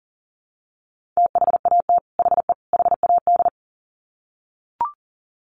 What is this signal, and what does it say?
Morse code, heard as a keyed tone, about 38 wpm, 700 Hz. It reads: THAT HE HAD
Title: Sets of 3 Words 40wpm